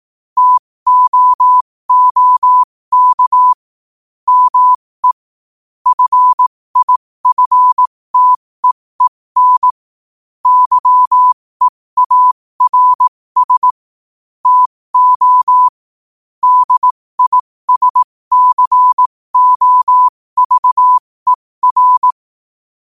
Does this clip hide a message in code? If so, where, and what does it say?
Yes: Morse code from 0.4 to 22.1 s, heard as a keyed tone, about 18 wpm, spaced as for 15 wpm, 1000 Hz. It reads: TOOK ME FIFTEEN YEARS TO DISCOVER